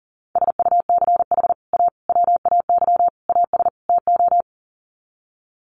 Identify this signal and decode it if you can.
Morse code, heard as a keyed tone, about 40 wpm, 700 Hz. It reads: SUCH A WAY AS TO
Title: Sets of 5 Words Encore2 40wpm